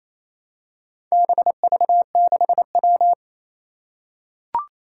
Morse code, heard as a keyed tone, about 29 wpm, 700 Hz. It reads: BV6W
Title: Four Challenging Letter-Number Combinations 28wpm